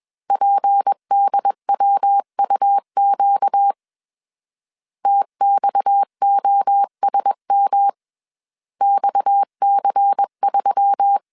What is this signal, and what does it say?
Morse code, heard as a keyed tone, about 21 wpm, 800 Hz. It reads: PDWUQ TXOSM XC3
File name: CW / DET / 1 6 1500C S C 20wpm 2116